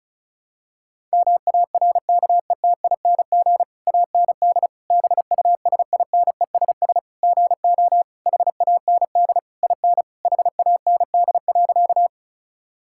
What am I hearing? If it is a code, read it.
Morse code, heard as a keyed tone, about 36 wpm, 690 Hz. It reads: MARKETING AND BUSINESS GO HAND IN HAND.
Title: Sentences from Top 1000 Words - Review Part 5 - 35wpm